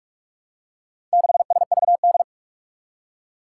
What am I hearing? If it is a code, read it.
Morse code, heard as a keyed tone, about 46 wpm, 700 Hz. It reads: 6SVD
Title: Four Challenging Letter-Number Combinations 45wpm